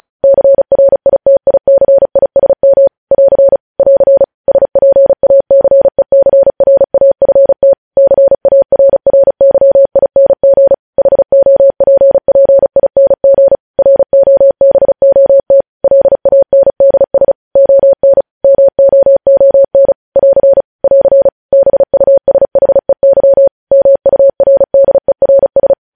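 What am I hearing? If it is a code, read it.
Morse code, heard as a keyed tone, about 36 wpm, 550 Hz. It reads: CRITICISM <AR> <AR> SPACECRAFT CARRYING HOPPING ROBOT LANDS ON MOON <AR> <AR> BUSHEY MURDERS
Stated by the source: News headlines sent in Morse code at 35 WPM.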